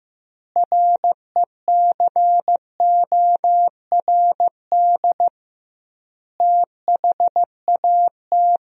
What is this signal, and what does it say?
Morse code, heard as a keyed tone, about 15 wpm, 700 Hz. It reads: RECORD THAT
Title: 2x Word-Spacing - Sentences from Top 500 Words - 15wpm